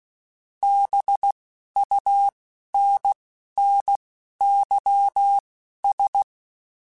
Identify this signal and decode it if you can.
Morse code, heard as a keyed tone, about 16 wpm, spaced as for 12 wpm, 800 Hz. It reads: BUNNYS